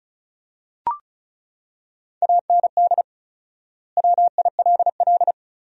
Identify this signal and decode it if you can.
Morse code, heard as a keyed tone, about 36 wpm, 700 Hz. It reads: AND WILL
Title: 4x Word-Spacing - Sentences from Top 400 Words - 35wpm